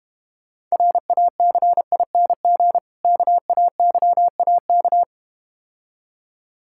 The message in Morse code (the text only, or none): RACING KAYAK